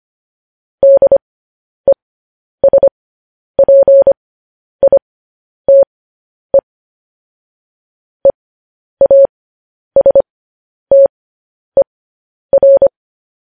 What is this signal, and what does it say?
Morse code, heard as a keyed tone, about 25 wpm, spaced as for 10 wpm, 550 Hz. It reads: DESPITE EASTER
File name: podcast / News headlines-1x-25x10